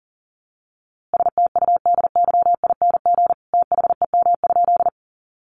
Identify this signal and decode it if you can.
Morse code, heard as a keyed tone, about 40 wpm, 700 Hz. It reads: STUDYING THEM?
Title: A Sound Of Thunder 40wpm